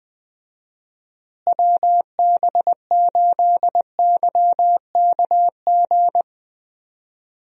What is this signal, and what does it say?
Morse code, heard as a keyed tone, about 21 wpm, 700 Hz. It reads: WB8YKG
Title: USA Call Signs - 6 Characters - Repeated 2x - 20wpm